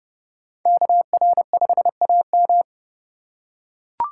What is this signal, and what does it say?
Morse code, heard as a keyed tone, about 30 wpm, 700 Hz. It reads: KR5AM